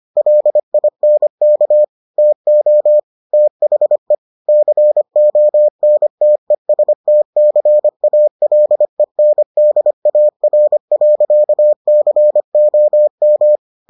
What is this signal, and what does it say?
Morse code, heard as a keyed tone, about 25 wpm, 600 Hz. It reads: LINK TO THE CONTESTCALENDAR.COM